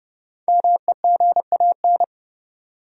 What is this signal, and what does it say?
Morse code, heard as a keyed tone, about 30 wpm, 700 Hz. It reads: MEGAN
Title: USA Common Names - Max 5 Letters - 30wpm